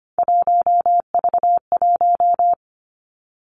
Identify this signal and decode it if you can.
Morse code, heard as a keyed tone, about 25 wpm, 700 Hz. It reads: 1V1
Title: Three Challenging Letter-Number Combinations 25wpm